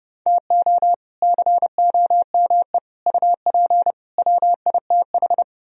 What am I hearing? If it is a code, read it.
Morse code, heard as a keyed tone, about 29 wpm, 700 Hz. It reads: TO COME UP WITH